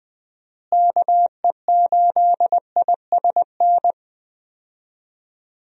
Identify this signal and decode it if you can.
Morse code, heard as a keyed tone, about 20 wpm, 700 Hz. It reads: KE8ISN
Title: USA Call Signs - 6 Characters - Repeated 2x - 20wpm